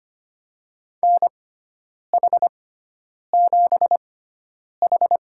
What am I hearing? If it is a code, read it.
Morse code, heard as a keyed tone, about 25 wpm, spaced as for 9 wpm, 700 Hz. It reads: NH7H
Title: USA Call Signs - 4 Characters - 6x Character Spacing - 25wpm